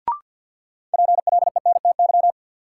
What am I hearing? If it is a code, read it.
Morse code, heard as a keyed tone, about 47 wpm, 700 Hz. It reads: PLENTY
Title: Sentences from Top 1300 Words - New Word Drill Alt - 50wpm